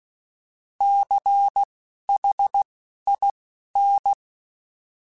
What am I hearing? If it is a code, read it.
Morse code, heard as a keyed tone, about 16 wpm, spaced as for 11 wpm, 800 Hz. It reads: CHIN
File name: Alphabetical / Alphabetical mp3s / Cs @10